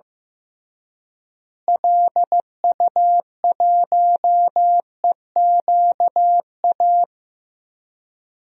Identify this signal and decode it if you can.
Morse code, heard as a keyed tone, about 16 wpm, 700 Hz. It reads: LU1EQA
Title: DXCC-100 Call Signs - 6 Characters - Repeated 3x - 15wpm